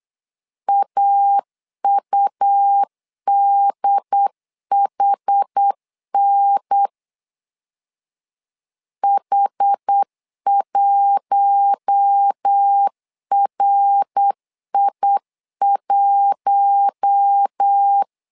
Morse code, heard as a keyed tone, about 8 wpm, 800 Hz. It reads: AUDHN H1RI1
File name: LEZIONI / 4 Q1 8wpm 3035